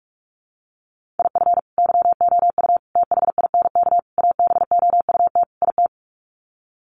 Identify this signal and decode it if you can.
Morse code, heard as a keyed tone, about 44 wpm, 700 Hz. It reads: IF YOU THINK ABOUT IT
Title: Sets of 5 Words Encore2 45wpm